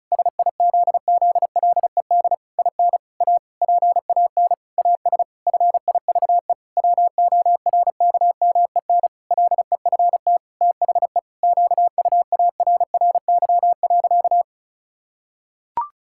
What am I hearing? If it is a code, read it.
Morse code, heard as a keyed tone, about 36 wpm, 700 Hz. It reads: SIZZLED IN A PAN AS FIVE WORKMEN LEFT THE QUARRY.
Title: Pangram Challenge v2 - 35wpm